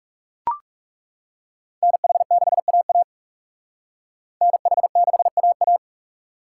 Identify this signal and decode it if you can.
Morse code, heard as a keyed tone, about 37 wpm, 700 Hz. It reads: NH6AA NH6AA
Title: Speed Racing - USA 2x2 Call Signs 30wpm